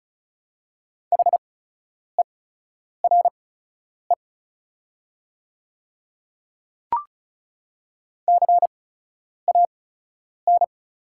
Morse code, heard as a keyed tone, about 34 wpm, spaced as for 10 wpm, 690 Hz. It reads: HERE CAN
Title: 8x Character Spacing - Top 200 Words - 35wpm